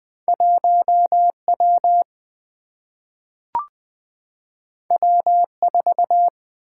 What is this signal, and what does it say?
Morse code, heard as a keyed tone, about 20 wpm, 700 Hz. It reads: 1W W4
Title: Single-Letter Single-Number 20wpm